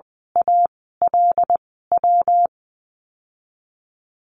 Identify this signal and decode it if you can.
Morse code, heard as a keyed tone, about 20 wpm, spaced as for 14 wpm, 700 Hz. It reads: ALW
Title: 128 Lesson 21g - Number 6 - Call Signs 20wpm